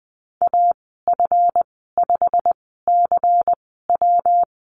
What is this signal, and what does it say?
Morse code, heard as a keyed tone, about 20 wpm, spaced as for 14 wpm, 700 Hz. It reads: AF5CW
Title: 140 Lesson 23g - Letter f - Call Signs 20wpm